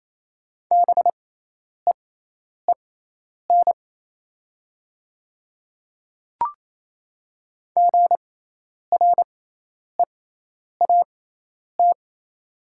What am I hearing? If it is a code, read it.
Morse code, heard as a keyed tone, about 27 wpm, spaced as for 10 wpm, 700 Hz. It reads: BEEN GREAT
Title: 6x Character Spacing - Top 300 Words - 28wpm